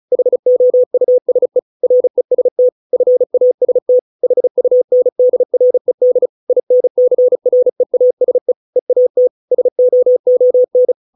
Morse code, heard as a keyed tone, about 36 wpm, 500 Hz. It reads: HOUSE REST FAST HUNDRED INCREASE EAT SOON
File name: 632 words / 632 Words 35WPM